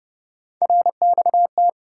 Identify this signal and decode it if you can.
Morse code, heard as a keyed tone, about 30 wpm, 700 Hz. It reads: RXT
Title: USA Call Sign Postfixes - 3 Letters 30wpm